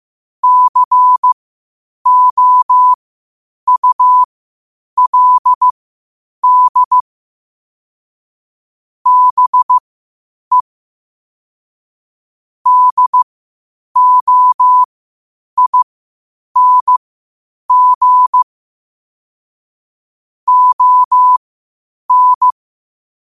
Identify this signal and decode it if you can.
Morse code, heard as a keyed tone, about 15 wpm, spaced as for 8 wpm, 1000 Hz. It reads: COULD BE DOING ON
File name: qotdfiles / 8wpm